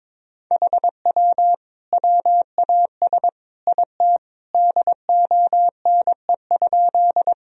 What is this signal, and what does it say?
Morse code, heard as a keyed tone, about 22 wpm, 700 Hz. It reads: HW WAS IT DONE?
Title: Sentences from Top 500 Words and Abbreviations 22wpm